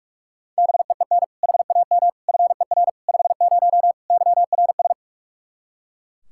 Morse code, heard as a keyed tone, about 41 wpm, 700 Hz. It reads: BEEN HAM FER 50 YRS
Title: Practice QSO - Exchange 3 - More Info - 45wpm